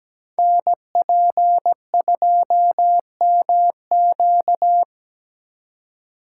Morse code, heard as a keyed tone, about 17 wpm, 700 Hz. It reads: NP2MQ
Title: Code Groups - 5 Letter:Numbers - 17wpm